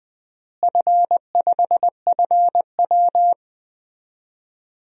Morse code, heard as a keyed tone, about 20 wpm, 700 Hz. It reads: F5FW
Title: DXCC-100 Call Signs - 4 Characters - Repeated 3x - 20wpm